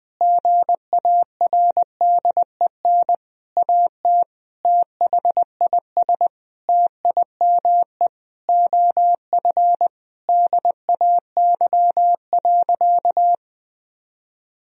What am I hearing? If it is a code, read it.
Morse code, heard as a keyed tone, about 21 wpm, 700 Hz. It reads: GARDEN AT THIS TIME OF DAY.
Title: Sentences from Top 1000 Words - Review Part 4 - 20wpm